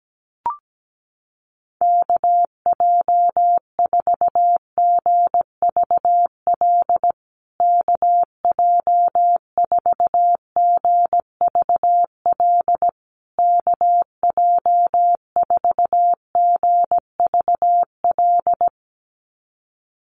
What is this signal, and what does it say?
Morse code, heard as a keyed tone, about 17 wpm, 700 Hz. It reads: KJ4GVL KJ4GVL KJ4GVL
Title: USA Call Signs Repeated 3x 17wpm